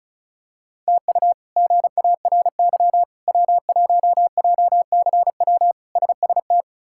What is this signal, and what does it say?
Morse code, heard as a keyed tone, about 36 wpm, 700 Hz. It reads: TU GARY W1JCW SST
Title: K1USN SST QSO Practice - 35wpm